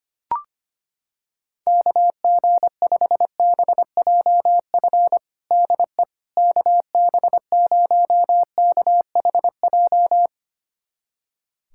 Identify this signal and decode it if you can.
Morse code, heard as a keyed tone, about 25 wpm, 700 Hz. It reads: KG5BJF DE KB0KHJ
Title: USA Callsign DE Callsign - 6 Characters - 25wpm